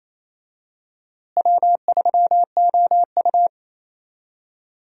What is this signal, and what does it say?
Morse code, heard as a keyed tone, about 27 wpm, 700 Hz. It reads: W3OU
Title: USA Call Signs - 4 Characters - Repeated 2x - 28wpm